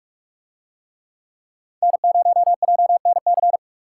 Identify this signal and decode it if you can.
Morse code, heard as a keyed tone, about 46 wpm, 700 Hz. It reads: N0JNC